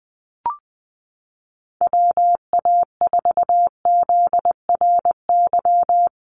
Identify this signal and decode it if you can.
Morse code, heard as a keyed tone, about 20 wpm, 700 Hz. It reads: WA4ZRY
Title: USA Call Signs 20wpm